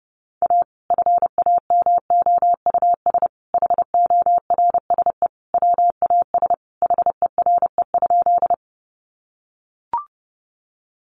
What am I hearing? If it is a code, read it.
Morse code, heard as a keyed tone, about 28 wpm, 700 Hz. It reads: A FAMOUS HORSE WAS HERE?
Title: Sentences from Top 1500 Words 30wpm